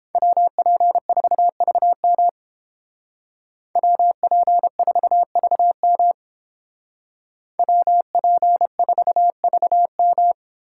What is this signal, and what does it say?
Morse code, heard as a keyed tone, about 29 wpm, 700 Hz. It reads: WP4VM WP4VM WP4VM